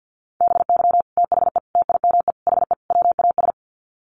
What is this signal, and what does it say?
Morse code, heard as a keyed tone, about 50 wpm, 700 Hz. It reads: BY THE TIME HE WAS
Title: Sets of 5 Words Encore 50wpm